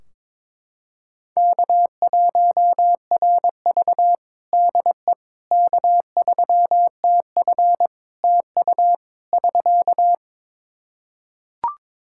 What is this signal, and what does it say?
Morse code, heard as a keyed tone, about 23 wpm, 700 Hz. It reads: K1RV DE K3TF TU <SK>
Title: Practice QSO - Exchange 4 - Ending - 22wpm